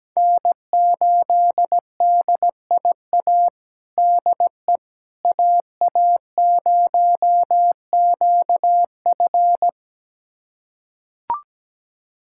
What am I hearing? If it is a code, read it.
Morse code, heard as a keyed tone, about 17 wpm, 700 Hz. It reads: N8DIA DE AA0QF
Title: USA Callsign DE Callsign - 5 Characters - 17wpm